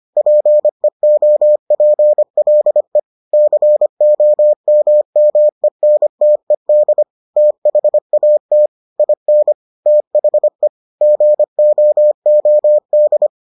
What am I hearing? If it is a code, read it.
Morse code, heard as a keyed tone, about 25 wpm, 600 Hz. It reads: PEOPLE COMMENTED THAT IN THE GOOD